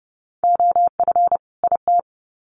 Morse code, heard as a keyed tone, about 30 wpm, 700 Hz. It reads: OF IT
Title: Sets of 2 Words 30wpm